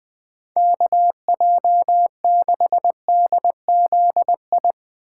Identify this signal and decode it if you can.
Morse code, heard as a keyed tone, about 22 wpm, 700 Hz. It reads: KJ6DZI
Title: Speed Racing - USA Call Signs 20wpm